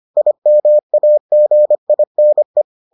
Morse code, heard as a keyed tone, about 25 wpm, 600 Hz. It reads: IMAGINE